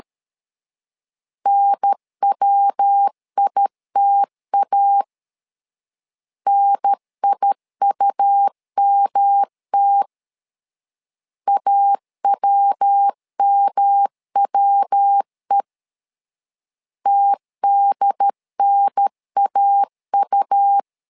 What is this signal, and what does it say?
Morse code, heard as a keyed tone, about 13 wpm, 800 Hz. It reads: NWITA NIUMT AWMWE TDNAU
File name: LEZIONI / 3 MWRO / 3 MW 12wpm 1633